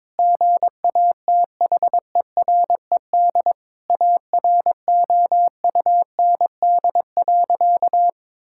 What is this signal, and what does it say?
Morse code, heard as a keyed tone, about 22 wpm, 700 Hz. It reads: GATHERED AROUND.
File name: practice / 22wpm / Sentences with 2-Word Phrases - 22wpm